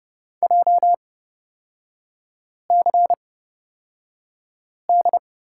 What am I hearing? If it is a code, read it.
Morse code, heard as a keyed tone, about 31 wpm, spaced as for 5 wpm, 690 Hz. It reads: JCD
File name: course / 30:10wpm / 236 Lesson 38e - Letter q - Three Characters 30wpm